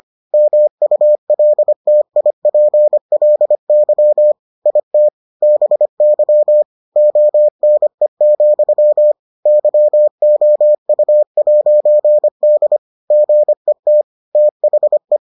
Morse code, heard as a keyed tone, about 25 wpm, 600 Hz. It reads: MULTIPLY IT BY ONE, YOU'D GET THE